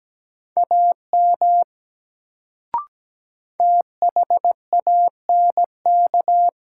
Morse code, heard as a keyed tone, about 17 wpm, 700 Hz. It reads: AM THANK
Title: Sentences from Top 600 Words - New Word Drill - 17wpm